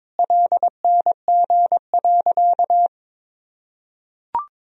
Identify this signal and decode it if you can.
Morse code, heard as a keyed tone, about 22 wpm, 700 Hz. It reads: LNG.
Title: Sentences from Top 500 Words and Abbreviations 22wpm